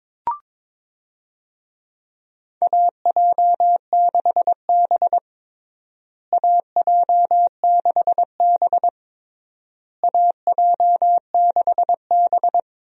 Morse code, heard as a keyed tone, about 22 wpm, 700 Hz. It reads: AJ6B AJ6B AJ6B
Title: USA Call Signs - 4 Characters - Repeated 3x - 22wpm